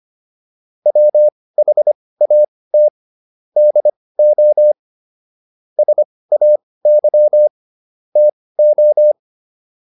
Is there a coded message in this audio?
Yes: Morse code from 0.9 to 9.1 s, heard as a keyed tone, about 25 wpm, spaced as for 18 wpm, 600 Hz. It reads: WHAT DO SAY TO